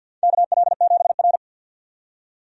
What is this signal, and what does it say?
Morse code, heard as a keyed tone, about 53 wpm, 700 Hz. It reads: KL7R
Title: USA Call Signs - 4 Characters - 50wpm